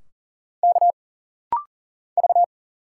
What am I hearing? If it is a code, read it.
Morse code, heard as a keyed tone, about 40 wpm, 700 Hz. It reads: K V
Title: Single Letters - Rapid-Fire 40wpm